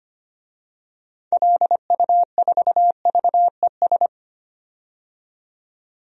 Morse code, heard as a keyed tone, about 25 wpm, 700 Hz. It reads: LU4VES
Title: DXCC-100 Call Signs - 6 Characters - 25wpm